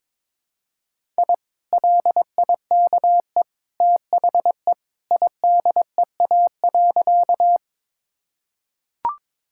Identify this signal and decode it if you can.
Morse code, heard as a keyed tone, about 22 wpm, 700 Hz. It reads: I LIKE THE IDEA.
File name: practice / 22wpm / Sentences with 2-Word Phrases - 22wpm